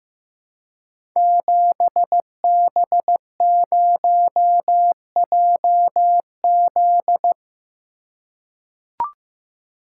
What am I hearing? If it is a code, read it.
Morse code, heard as a keyed tone, about 15 wpm, 700 Hz. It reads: 7B0JZ
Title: Code Groups - 5 Letter:Numbers - 15wpm